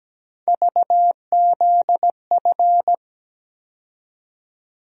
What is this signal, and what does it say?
Morse code, heard as a keyed tone, about 17 wpm, 700 Hz. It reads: VZF
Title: USA Call Sign Postfixes - 3 Letters 17wpm